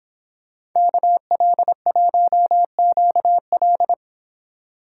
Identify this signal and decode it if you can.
Morse code, heard as a keyed tone, about 26 wpm, 700 Hz. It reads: KL1QL